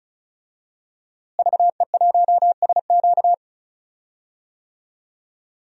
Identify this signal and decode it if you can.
Morse code, heard as a keyed tone, about 34 wpm, 700 Hz. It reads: VE1SQ
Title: DXCC-100 Call Signs - 5 Characters - Repeated 2x - 35wpm